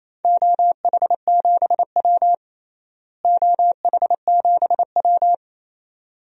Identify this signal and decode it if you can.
Morse code, heard as a keyed tone, about 28 wpm, 700 Hz. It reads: OH7W OH7W
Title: DXCC-100 Call Signs - 4 Characters - Repeated 2x - 28wpm